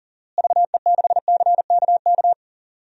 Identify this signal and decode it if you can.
Morse code, heard as a keyed tone, about 40 wpm, 700 Hz. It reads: VE6CKK
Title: DXCC-100 Call Signs - 6 Characters - Repeated 3x - 40wpm